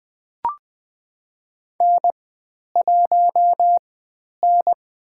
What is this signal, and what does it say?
Morse code, heard as a keyed tone, about 20 wpm, 700 Hz. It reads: N 1 N
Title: 095 Lesson 16g - Letter c - Call Signs 20wpm